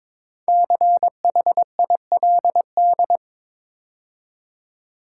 Morse code, heard as a keyed tone, about 22 wpm, 700 Hz. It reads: CHILD
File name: 5 Letter Words - First 3 Letters Given - 22wpm